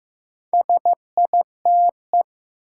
Morse code, heard as a keyed tone, about 15 wpm, 700 Hz. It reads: SITE
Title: Sentences from Top 1000 Words - Review Part 1 - New Word Drill - 15wpm